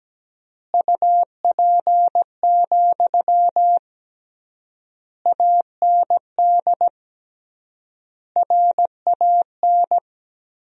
Morse code, heard as a keyed tone, about 17 wpm, 700 Hz. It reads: UP, AND RAN
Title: 3x Word-Spacing - Sentences from Top 500 Words - 17wpm